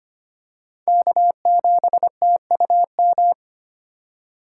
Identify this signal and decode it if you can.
Morse code, heard as a keyed tone, about 25 wpm, 700 Hz. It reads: K7TUM
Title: USA Call Signs - 3x Character Spacing - 25wpm